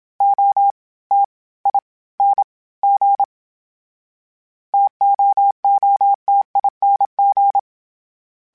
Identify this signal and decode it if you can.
Morse code, heard as a keyed tone, about 27 wpm, 800 Hz. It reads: O T I N G TOOTING
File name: Variable Speed / Variable Speed-mp3s / T AOEI's sp @25